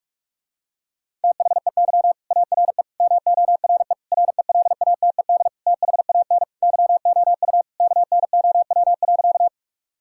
Morse code, heard as a keyed tone, about 46 wpm, 700 Hz. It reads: THEY ARE MORE RELATED THAN YOU KNOW.
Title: Sentences from Top 1000 Words - Review Part 5 - 45wpm